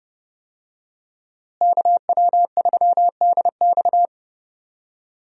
Morse code, heard as a keyed tone, about 32 wpm, 700 Hz. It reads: KW3DX
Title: USA Call Signs - 5 Characters - Repeated 3x - 30wpm